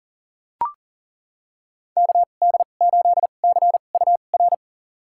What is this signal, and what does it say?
Morse code, heard as a keyed tone, about 40 wpm, spaced as for 29 wpm, 700 Hz. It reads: KD8CUR
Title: USA Call Signs - 6 Characters - 2x Character Spacing - 40wpm